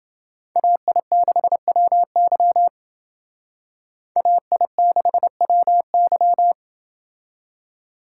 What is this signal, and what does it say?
Morse code, heard as a keyed tone, about 28 wpm, 700 Hz. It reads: AI6WY AI6WY